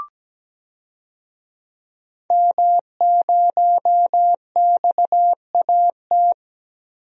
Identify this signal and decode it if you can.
Morse code, heard as a keyed tone, about 17 wpm, 700 Hz. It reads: M0XAT
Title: DXCC-100 Call Signs - 5 Characters - 17wpm